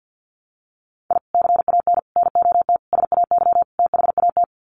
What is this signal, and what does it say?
Morse code, heard as a keyed tone, about 50 wpm, 700 Hz. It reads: I CAN NOT SAY THAT
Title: Sets of 5 Words Encore2 50wpm